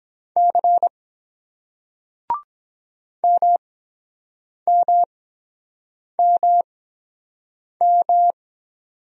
Morse code, heard as a keyed tone, about 23 wpm, spaced as for 7 wpm, 700 Hz. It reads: C MMMM